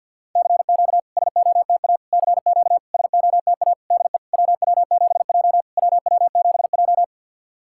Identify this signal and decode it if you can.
Morse code, heard as a keyed tone, about 53 wpm, 690 Hz. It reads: CQ SOTA CQ SOTA DE WW7J WW7J
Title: SOTA QSO Practice 50wpm